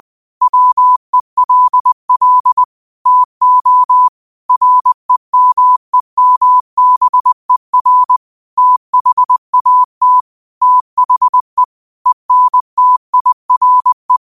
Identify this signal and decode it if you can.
Morse code, heard as a keyed tone, about 20 wpm, 1000 Hz. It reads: WELL TO REMEMBER THAT THE ENTIRE
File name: qotdfiles / 20wpm